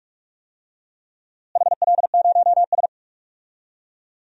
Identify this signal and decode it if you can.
Morse code, heard as a keyed tone, about 46 wpm, 700 Hz. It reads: HL0S
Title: DXCC-100 Call Signs - 4 Characters - Repeated 3x - 45wpm